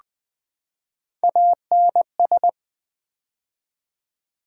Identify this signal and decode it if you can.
Morse code, heard as a keyed tone, about 20 wpm, 700 Hz. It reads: ANS
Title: Build A Word - 3 Letters At A Time - 6-Letter Words - 20wpm